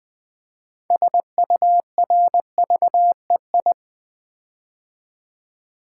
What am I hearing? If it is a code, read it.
Morse code, heard as a keyed tone, about 20 wpm, 700 Hz. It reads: SURVEI
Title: Build A Word - 6 Letters At A Time - 12-Letter Words - 20wpm